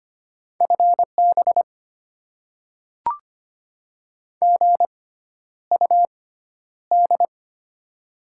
Morse code, heard as a keyed tone, about 25 wpm, spaced as for 11 wpm, 700 Hz. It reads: FB GUD
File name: Top 20 QSO Elements - 6x Character Spacing - 25wpm